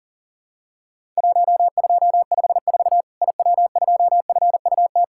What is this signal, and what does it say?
Morse code, heard as a keyed tone, about 42 wpm, 700 Hz. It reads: 1254 IW2FUT
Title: Worked All Europe DX Contest - QTC Exchange - 40wpm